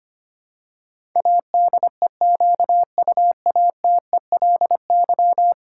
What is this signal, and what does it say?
Morse code, heard as a keyed tone, about 25 wpm, 700 Hz. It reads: ADEQUATELY
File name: 10 Letter Words - First 3 Letters Given - 25wpm